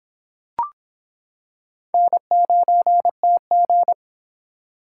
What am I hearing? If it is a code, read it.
Morse code, heard as a keyed tone, about 26 wpm, 700 Hz. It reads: N9TG